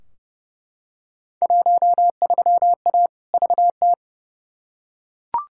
Morse code, heard as a keyed tone, about 30 wpm, 700 Hz. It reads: 13A VT
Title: Field Day Exchanges Encore - 30wpm